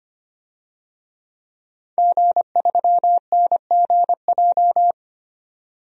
Morse code, heard as a keyed tone, about 25 wpm, 700 Hz. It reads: G3NGJ
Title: DXCC-100 Call Signs - 5 Characters - 25wpm